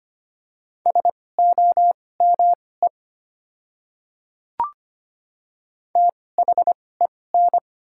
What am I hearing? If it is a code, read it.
Morse code, heard as a keyed tone, about 25 wpm, spaced as for 18 wpm, 690 Hz. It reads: SOME THEN
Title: 2x Character Spacing - Top 100 Words - 25wpm